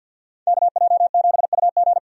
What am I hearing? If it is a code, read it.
Morse code, heard as a keyed tone, about 50 wpm, 700 Hz. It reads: KJ7UG
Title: USA Call Signs - 5 Characters - 50wpm